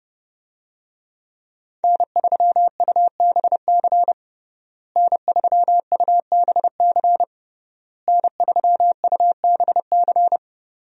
Morse code, heard as a keyed tone, about 30 wpm, 700 Hz. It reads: N3UBC N3UBC N3UBC
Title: USA Call Signs - 5 Characters - Repeated 3x - 30wpm